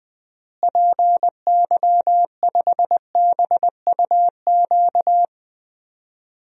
Morse code, heard as a keyed tone, about 20 wpm, 700 Hz. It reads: PY5BUQ
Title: DXCC-100 Call Signs - 6 Characters - Repeated 3x - 20wpm